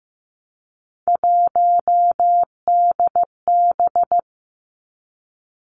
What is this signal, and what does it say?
Morse code, heard as a keyed tone, about 15 wpm, 700 Hz. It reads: 1DB
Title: Three Challenging Letter-Number Combinations 15wpm